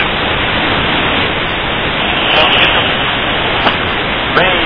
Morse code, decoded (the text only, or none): none